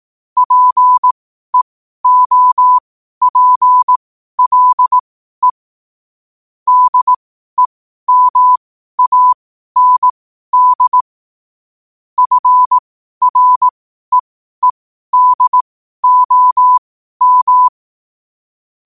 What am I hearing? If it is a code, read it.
Morse code, heard as a keyed tone, about 18 wpm, spaced as for 12 wpm, 1000 Hz. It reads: PEOPLE DEMAND FREEDOM
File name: qotdfiles / 12wpm